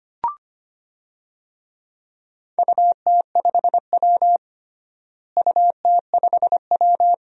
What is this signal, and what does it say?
Morse code, heard as a keyed tone, about 25 wpm, 700 Hz. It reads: UT5W UT5W
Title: DXCC-100 Call Signs - 4 Characters - Repeated 2x - 25wpm